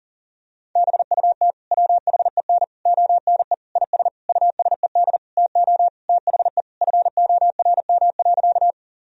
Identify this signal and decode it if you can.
Morse code, heard as a keyed tone, about 40 wpm, 700 Hz. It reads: BUT WHEN ONE IS USED TO THE FORM.
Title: Sentences from Top 1000 Words - Review Part 1 - 40wpm